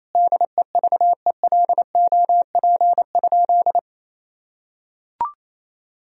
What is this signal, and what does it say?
Morse code, heard as a keyed tone, about 28 wpm, 700 Hz. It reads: DEVELOP?
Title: Sentences from Top 1000 Words - Review Part 2 - 28wpm